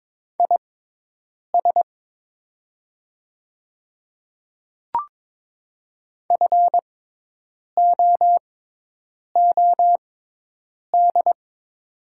Morse code, heard as a keyed tone, about 22 wpm, spaced as for 8 wpm, 700 Hz. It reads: IS FOOD
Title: 6x Character Spacing - Top 500 Words - 22wpm